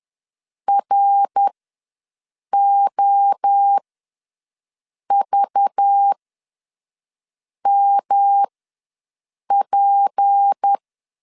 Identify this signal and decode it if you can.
Morse code, heard as a keyed tone, about 10 wpm, spaced as for 5 wpm, 800 Hz. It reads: ROVMP
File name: LEZIONI / 6 PKX0 / 6 P ws 10wpm 3751